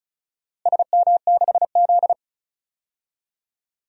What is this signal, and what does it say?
Morse code, heard as a keyed tone, about 34 wpm, 700 Hz. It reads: SM6Z
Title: DXCC-100 Call Signs - 4 Characters - Repeated 2x - 35wpm